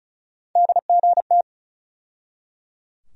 Morse code, heard as a keyed tone, about 34 wpm, 700 Hz. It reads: DGT
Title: USA Call Sign Postfixes - 3 Letters 35wpm